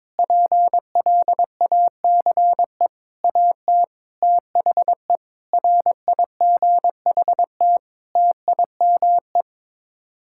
Morse code, heard as a keyed tone, about 22 wpm, 700 Hz. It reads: PLACE AT THE RIGHT TIME
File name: practice / 22wpm / Sets of 8 Words - 22wpm